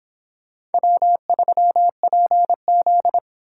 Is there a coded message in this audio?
Yes: Morse code from 0.7 to 3.2 s, heard as a keyed tone, about 30 wpm, 700 Hz. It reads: W3PZ